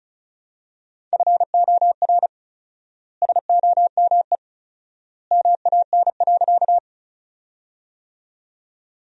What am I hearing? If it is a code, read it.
Morse code, heard as a keyed tone, about 36 wpm, 700 Hz. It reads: FOR SOME MAN.
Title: 4x Word-Spacing - Sentences from Top 200 Words - 35wpm